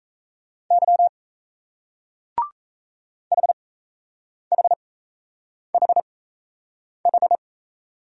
Morse code, heard as a keyed tone, about 38 wpm, spaced as for 8 wpm, 700 Hz. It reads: Y HHHH